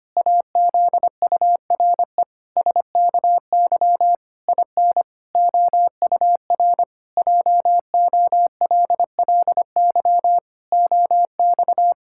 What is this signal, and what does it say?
Morse code, heard as a keyed tone, about 25 wpm, 700 Hz. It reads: AZURE SKY IN OUR JOLLY OX
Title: Pangram Challenge v2 - 25wpm